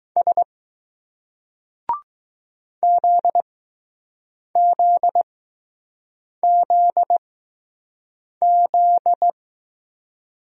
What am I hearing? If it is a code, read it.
Morse code, heard as a keyed tone, about 20 wpm, spaced as for 6 wpm, 700 Hz. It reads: S ZZZZ